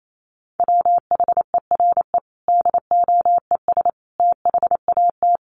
Morse code, heard as a keyed tone, about 28 wpm, 700 Hz. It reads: WHERE DOES THAT
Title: Sentences from Top 1200 Words 28wpm